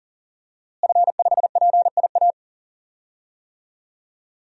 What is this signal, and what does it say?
Morse code, heard as a keyed tone, about 38 wpm, 700 Hz. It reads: F5PIA
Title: DXCC-100 Call Signs - 5 Characters - Repeated 3x - 40wpm